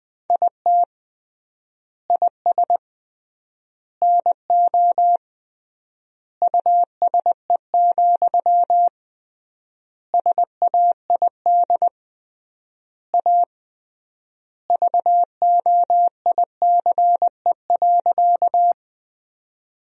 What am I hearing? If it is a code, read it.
Morse code, heard as a keyed tone, about 21 wpm, 700 Hz. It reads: IT IS NO USE, SAID A VOICE.
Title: 3x Word-Spacing - Sentences from Top 500 Words - 20wpm